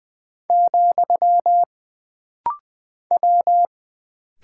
Morse code, heard as a keyed tone, about 20 wpm, 700 Hz. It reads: , W
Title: Single Character (Letter:Number:Punctuation:Prosign) - Rapid-Fire - 20wpm